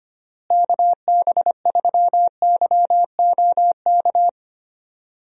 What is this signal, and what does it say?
Morse code, heard as a keyed tone, about 25 wpm, 700 Hz. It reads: KB3YOK
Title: USA Call Signs - 6 Characters - Repeated 3x - 25wpm